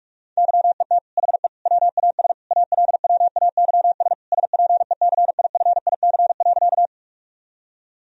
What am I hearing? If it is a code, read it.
Morse code, heard as a keyed tone, about 43 wpm, 700 Hz. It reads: YET HE WAS ALWAYS SPECIFIC.
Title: Sentences from Top 1000 Words - Review Part 3 - 45wpm